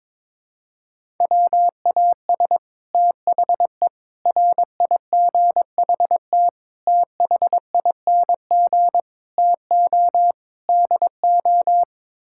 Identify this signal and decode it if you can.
Morse code, heard as a keyed tone, about 22 wpm, 700 Hz. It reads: WAS THE RIGHT THING TO DO
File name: practice / 22wpm / Sets of 6 Words - 22wpm